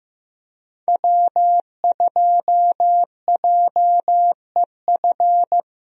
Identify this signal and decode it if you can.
Morse code, heard as a keyed tone, about 15 wpm, 700 Hz. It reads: W2JEF
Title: USA Call Signs - 5 Characters - Repeated 2x - 15wpm